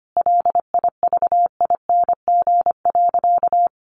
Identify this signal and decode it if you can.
Morse code, heard as a keyed tone, about 25 wpm, 700 Hz. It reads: LIVING.
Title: Sentences from Top 1500 Words 25wpm